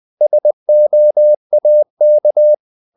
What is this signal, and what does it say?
Morse code, heard as a keyed tone, about 20 wpm, 600 Hz. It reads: SOAK
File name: the old man and the sea part 2 20wpm